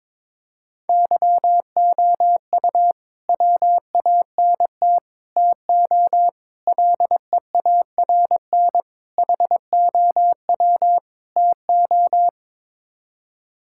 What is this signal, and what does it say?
Morse code, heard as a keyed tone, about 22 wpm, 700 Hz. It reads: YOU WANT TO LEARN HOW TO
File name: practice / 22wpm / Sets of 6 Words - 22wpm